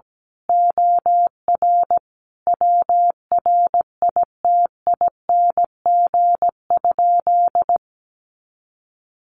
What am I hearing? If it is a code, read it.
Morse code, heard as a keyed tone, about 17 wpm, 700 Hz. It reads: OR WRITING?
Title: Sentences from Top 1200 Words 17wpm